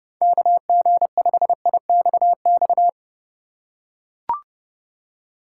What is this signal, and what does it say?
Morse code, heard as a keyed tone, about 30 wpm, 700 Hz. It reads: KG5IXX
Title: USA Call Signs - 6 Characters - 4x Character Spacing - 30wpm